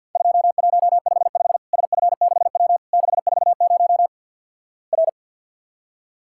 Morse code, heard as a keyed tone, about 50 wpm, 700 Hz. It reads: 2155 SF6W 640 R
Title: Worked All Europe DX Contest - QTC Exchange - 50wpm